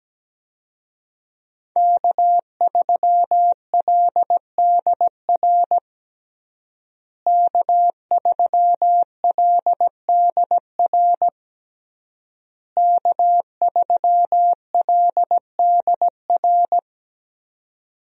Morse code, heard as a keyed tone, about 17 wpm, 700 Hz. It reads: K3LDR K3LDR K3LDR
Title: USA Call Signs - 5 Characters - Repeated 3x - 17wpm